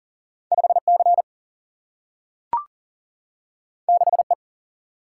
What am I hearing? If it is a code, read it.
Morse code, heard as a keyed tone, about 42 wpm, 700 Hz. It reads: 5C 6E
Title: Single-Letter Single-Number 40wpm